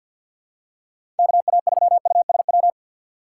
Morse code, heard as a keyed tone, about 51 wpm, 700 Hz. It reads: KA3USW
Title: USA Call Signs - 6 Characters - 4x Character Spacing - 50wpm